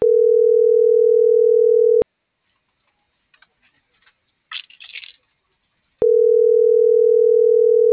Ringback tone (North America), 440 + 480 Hz.